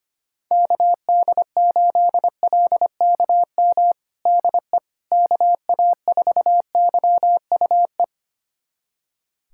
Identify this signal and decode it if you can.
Morse code, heard as a keyed tone, about 25 wpm, 700 Hz. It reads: KD8LKM DE KA4YUE
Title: USA Callsign DE Callsign - 6 Characters - 25wpm